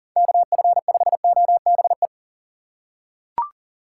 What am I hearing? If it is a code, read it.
Morse code, heard as a keyed tone, about 40 wpm, 700 Hz. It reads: KF5OBE
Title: USA Call Signs - 6 Characters - 8x Character Spacing - 40wpm